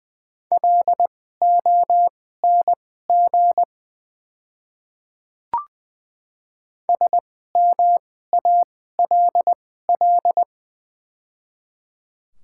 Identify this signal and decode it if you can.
Morse code, heard as a keyed tone, about 20 wpm, spaced as for 14 wpm, 700 Hz. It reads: LONG SMALL
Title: 2x Character Spacing - Top 300 Words - 20wpm